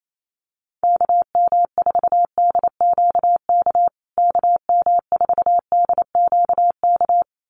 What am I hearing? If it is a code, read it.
Morse code, heard as a keyed tone, about 28 wpm, 700 Hz. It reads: KM4DQK KM4DQK
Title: USA Call Signs Repeated 2x 28wpm